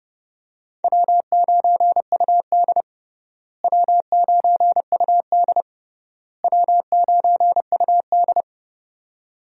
Morse code, heard as a keyed tone, about 32 wpm, 700 Hz. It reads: W9UD W9UD W9UD
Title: USA Call Signs - 4 Characters - Repeated 3x - 30wpm